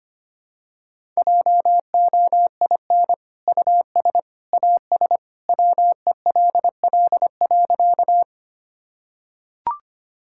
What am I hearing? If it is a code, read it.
Morse code, heard as a keyed tone, about 25 wpm, 700 Hz. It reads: JOIN US AS WELL.
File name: practice / 25wpm / Sentences with 2-Word Phrases - 25wpm